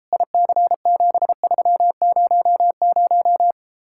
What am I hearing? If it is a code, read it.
Morse code, heard as a keyed tone, about 34 wpm, 700 Hz. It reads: IC7300